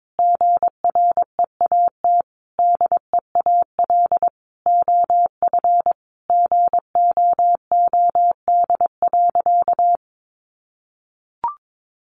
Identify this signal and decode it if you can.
Morse code, heard as a keyed tone, about 22 wpm, 700 Hz. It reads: GREAT DEAL OF GOOD.
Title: Sentences from Top 900 Words 22wpm